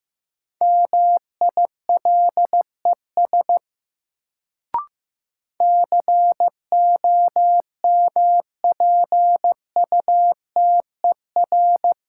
Morse code, heard as a keyed tone, about 15 wpm, 700 Hz. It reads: MILES COMPUTER
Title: Sentences from Top 1000 Words - Review Part 3 - New Word Drill - 15wpm